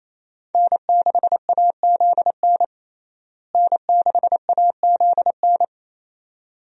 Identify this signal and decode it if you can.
Morse code, heard as a keyed tone, about 28 wpm, 700 Hz. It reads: N6AZN N6AZN
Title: USA Call Signs - 5 Characters - Repeated 2x - 28wpm